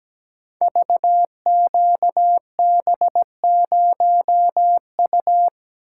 Morse code, heard as a keyed tone, about 17 wpm, 700 Hz. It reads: VQB0U
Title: Code Groups - 5 Letter:Numbers - 17wpm